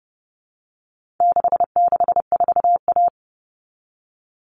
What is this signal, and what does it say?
Morse code, heard as a keyed tone, about 30 wpm, 700 Hz. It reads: 664A
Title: All Cut Numbers - 4-Digit 30wpm